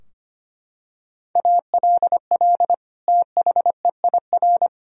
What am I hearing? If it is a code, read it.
Morse code, heard as a keyed tone, about 25 wpm, 700 Hz. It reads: ALL THEIR
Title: Sentences from Top 1000 Words - Review Part 2 - 25wpm